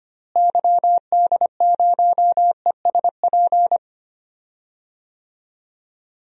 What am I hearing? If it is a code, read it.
Morse code, heard as a keyed tone, about 25 wpm, 700 Hz. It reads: YD0ESP
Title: DXCC-100 Call Signs - 6 Characters - Repeated 2x - 25wpm